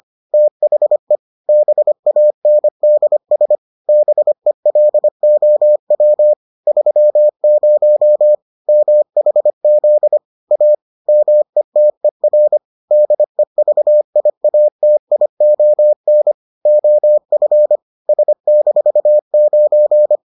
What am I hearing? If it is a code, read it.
Morse code, heard as a keyed tone, about 25 wpm, 600 Hz. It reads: THE BANDS BELOW 30 MHZ A METER DEVIATION OF S-9